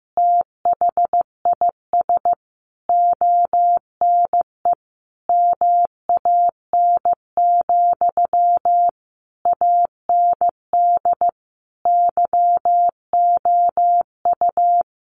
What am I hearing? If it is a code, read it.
Morse code, heard as a keyed tone, about 15 wpm, 700 Hz. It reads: THIS ONE MAN, AND YOU
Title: A Sound Of Thunder 15wpm